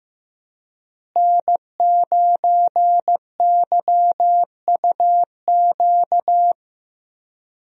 Morse code, heard as a keyed tone, about 16 wpm, 700 Hz. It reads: N9YUQ
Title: USA Call Signs - 5 Characters - Repeated 2x - 15wpm